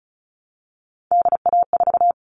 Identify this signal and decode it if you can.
Morse code, heard as a keyed tone, about 34 wpm, 700 Hz. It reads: DA4
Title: All Cut Numbers - 3-Digit 35wpm